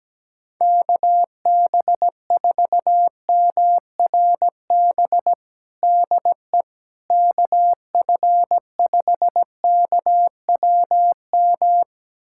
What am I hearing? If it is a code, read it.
Morse code, heard as a keyed tone, about 17 wpm, 700 Hz. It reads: KB4MRB DE KF5KWM
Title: USA Callsign DE Callsign 17wpm